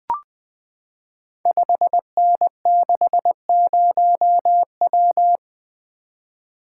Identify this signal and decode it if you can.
Morse code, heard as a keyed tone, about 21 wpm, 700 Hz. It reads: 5N60W
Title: Code Groups - 5 Letter:Numbers - 20wpm